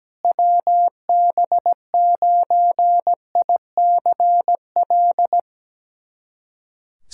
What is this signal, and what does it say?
Morse code, heard as a keyed tone, about 17 wpm, 700 Hz. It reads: WB9ICL
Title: SOTA QSO Practice 17wpm